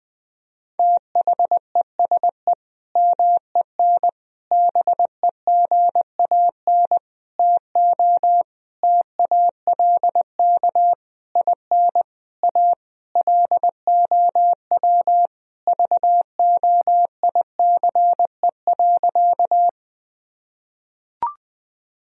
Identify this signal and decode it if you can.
Morse code, heard as a keyed tone, about 21 wpm, 700 Hz. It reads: THESE MEN BEGAN TO TALK IN A LOW VOICE.
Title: Sentences from Top 1000 Words - Review Part 1 - 20wpm